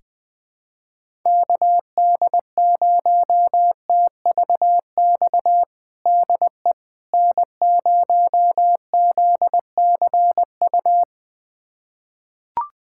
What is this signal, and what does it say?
Morse code, heard as a keyed tone, about 20 wpm, 700 Hz. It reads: KD0TVX DE N0ZCU
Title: USA Callsign DE Callsign 20wpm